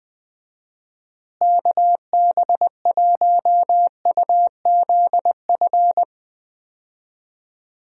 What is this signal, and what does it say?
Morse code, heard as a keyed tone, about 21 wpm, 700 Hz. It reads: KB1UZF
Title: USA Call Signs - 6 Characters - Repeated 2x - 20wpm